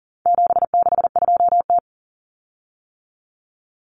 Morse code, heard as a keyed tone, about 40 wpm, 700 Hz. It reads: 762T
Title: Common Cut Numbers - 4-Digit 40wpm